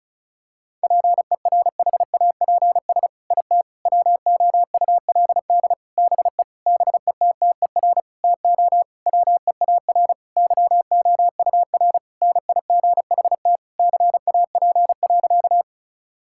Morse code, heard as a keyed tone, about 36 wpm, 700 Hz. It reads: PERHAPS IT WOULD BE BETTER TO WEAR YOUR NIGHT CAP.
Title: Sentences from Top 1600 Words 35wpm